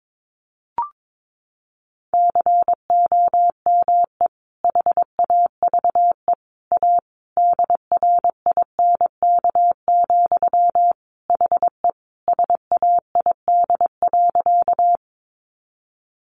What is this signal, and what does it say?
Morse code, heard as a keyed tone, about 22 wpm, 700 Hz. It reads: COME HAVE A DRINK, HE SAID.
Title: Sentences from Top 1500 Words 22wpm